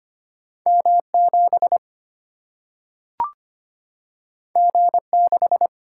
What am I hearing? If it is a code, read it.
Morse code, heard as a keyed tone, about 25 wpm, 700 Hz. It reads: M 7 G 6
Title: Single-Letter Single-Number 25wpm